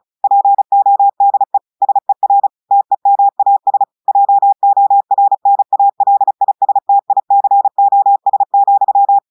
Morse code, heard as a keyed tone, about 36 wpm, 800 Hz. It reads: PODE SER TEMAS JORNALISTICOS,
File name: audios / portugais / portugais texte en clair 2 vitesse 35 mots minutes